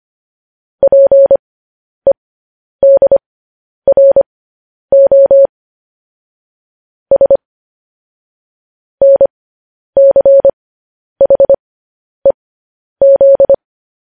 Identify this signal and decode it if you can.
Morse code, heard as a keyed tone, about 25 wpm, spaced as for 10 wpm, 550 Hz. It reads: PEDRO S NCHEZ